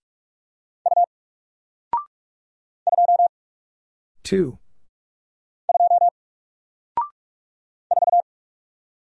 Morse code, heard as a keyed tone, about 44 wpm, 700 Hz. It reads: U 2 2 4
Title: Single Letter-Number Alt 45wpm